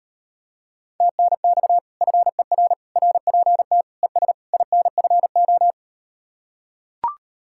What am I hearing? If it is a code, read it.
Morse code, heard as a keyed tone, about 38 wpm, 700 Hz. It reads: TNX FER RPT ES INFO